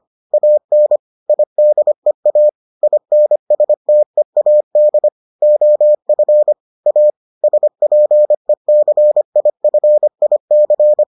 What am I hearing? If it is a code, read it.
Morse code, heard as a keyed tone, about 25 wpm, 600 Hz. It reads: AN IDEA INSTEAD OF A SPECIFIC